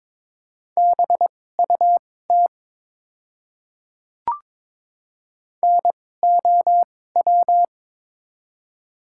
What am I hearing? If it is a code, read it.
Morse code, heard as a keyed tone, about 22 wpm, spaced as for 16 wpm, 700 Hz. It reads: BUT NOW
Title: 2x Character Spacing - Top 200 Words - 22wpm